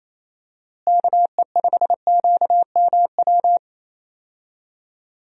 Morse code, heard as a keyed tone, about 28 wpm, 700 Hz. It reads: KE5QMW
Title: USA Call Signs - 6 Characters - Repeated 3x - 28wpm